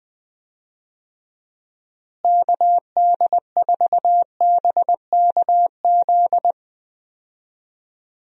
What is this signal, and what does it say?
Morse code, heard as a keyed tone, about 20 wpm, 700 Hz. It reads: KD4BKZ
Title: USA Call Signs - 6 Characters - Repeated 3x - 20wpm